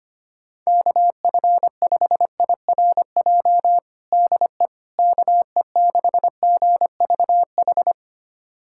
Morse code, heard as a keyed tone, about 25 wpm, 700 Hz. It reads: KF5IRJ DE KE6GVH
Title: USA Callsign DE Callsign - 6 Characters - 25wpm